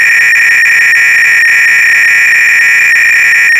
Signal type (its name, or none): 300-baud modem data